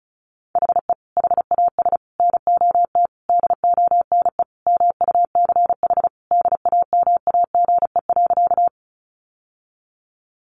Morse code, heard as a keyed tone, about 34 wpm, 700 Hz. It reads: HE HAS NOT DONE MUCH DAMAGE.
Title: Sentences from Top 1100 Words 35wpm